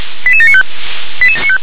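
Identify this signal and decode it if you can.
Five-tone selective call (ZVEI), heard as quick runs of five tones, tones of about 70 ms. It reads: 88795
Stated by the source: ZVEI five-tone call received by SDR.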